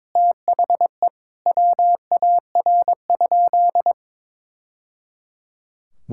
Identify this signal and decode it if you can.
Morse code, heard as a keyed tone, about 22 wpm, 700 Hz. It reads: THE WAR?
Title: Sentences from Top 1000 Words - Review Part 5 - 22wpm